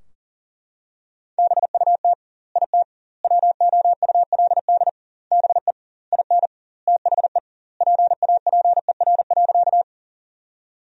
Morse code, heard as a keyed tone, about 40 wpm, 700 Hz. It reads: BUT IT WOULD BE IN THE PAPER.
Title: 2x Word-Spacing - Sentences from Top 300 Words - 40wpm